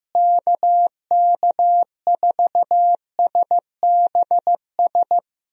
Morse code, heard as a keyed tone, about 15 wpm, 700 Hz. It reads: KK4SBS
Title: USA Call Signs - 6 Characters - Repeated 3x - 15wpm